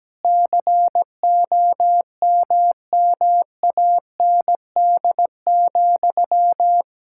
Morse code, heard as a keyed tone, about 17 wpm, 700 Hz. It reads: COMMAND,
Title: Sentences from Top 1600 Words 17wpm